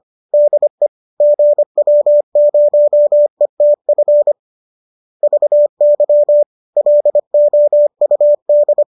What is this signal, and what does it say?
Morse code, heard as a keyed tone, about 25 wpm, 590 Hz. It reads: DE GW0ETF VY LOUD